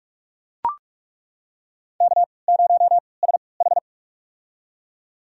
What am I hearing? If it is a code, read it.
Morse code, heard as a keyed tone, about 42 wpm, spaced as for 25 wpm, 700 Hz. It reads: K0SH
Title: USA Call Signs - 4 Characters - 3x Character Spacing - 45wpm